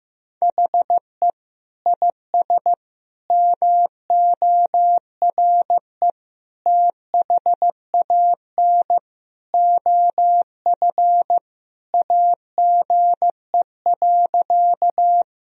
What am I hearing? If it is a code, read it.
Morse code, heard as a keyed tone, about 15 wpm, 700 Hz. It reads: HE IS MORE THAN OF AGE.
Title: Sentences from Top 1000 Words - Review Part 3 - 15wpm